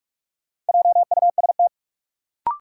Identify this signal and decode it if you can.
Morse code, heard as a keyed tone, about 44 wpm, 700 Hz. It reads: JUST
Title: Sentences from Top 200 Words - New Word Drill - 45wpm